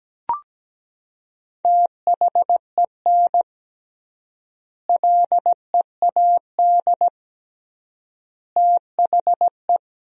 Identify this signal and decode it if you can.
Morse code, heard as a keyed tone, about 17 wpm, 700 Hz. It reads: THEN LEAD THE
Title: 3x Word-Spacing - Sentences from Top 500 Words - 17wpm